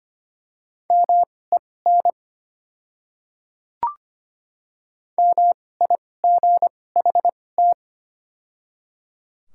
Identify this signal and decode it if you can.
Morse code, heard as a keyed tone, about 25 wpm, spaced as for 18 wpm, 700 Hz. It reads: MEN MIGHT
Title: 2x Character Spacing - Top 300 Words - 25wpm